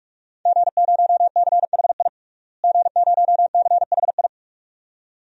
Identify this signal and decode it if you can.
Morse code, heard as a keyed tone, about 46 wpm, 700 Hz. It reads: G0CHI G0CHI
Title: DXCC-100 Call Signs - 5 Characters - Repeated 2x - 45wpm